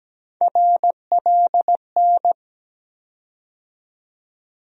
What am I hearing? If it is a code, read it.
Morse code, heard as a keyed tone, about 17 wpm, 700 Hz. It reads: RLN
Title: USA Call Sign Postfixes - 3 Letters 17wpm